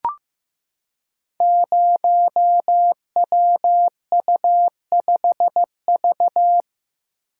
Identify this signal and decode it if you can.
Morse code, heard as a keyed tone, about 15 wpm, 700 Hz. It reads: 0WU5V
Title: Code Groups - 5 Letter:Numbers - 15wpm